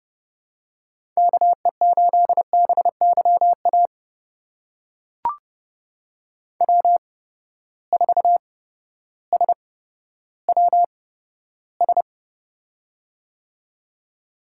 Morse code, heard as a keyed tone, about 30 wpm, spaced as for 14 wpm, 700 Hz. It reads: KE8BYA W4SWS
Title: USA Call Signs - 8x Character Spacing - 30wpm